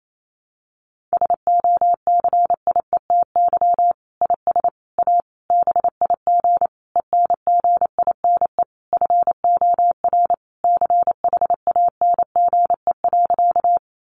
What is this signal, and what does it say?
Morse code, heard as a keyed tone, about 28 wpm, 700 Hz. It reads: SOCIETY IS A BIG ENGINE FOR CHANGE.
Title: Sentences from Top 1500 Words 28wpm